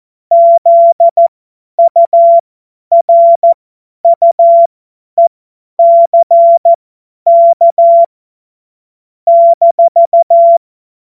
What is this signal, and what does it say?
Morse code, heard as a keyed tone, about 14 wpm, spaced as for 10 wpm, 680 Hz. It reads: ZURUECK -
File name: Morsekurs / Witze / Witz 09. August 2022 50 F70